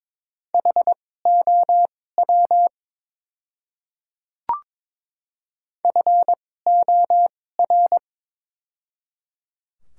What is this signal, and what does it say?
Morse code, heard as a keyed tone, about 22 wpm, spaced as for 16 wpm, 700 Hz. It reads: HOW FOR
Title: 2x Character Spacing - Top 100 Words - 22wpm